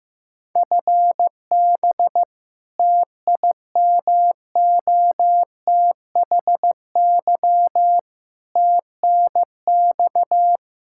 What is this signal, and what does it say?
Morse code, heard as a keyed tone, about 16 wpm, 700 Hz. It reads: FB TIMOTHY TNX
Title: K1USN SST QSO Practice 15wpm